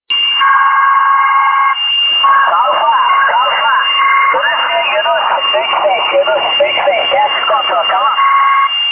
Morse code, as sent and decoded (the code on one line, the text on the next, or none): none